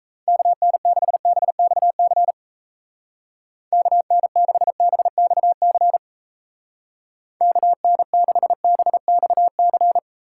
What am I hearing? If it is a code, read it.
Morse code, heard as a keyed tone, about 34 wpm, 700 Hz. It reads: KN6BXC KN6BXC KN6BXC
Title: Speed Racing - USA 2x3 Call Signs 28wpm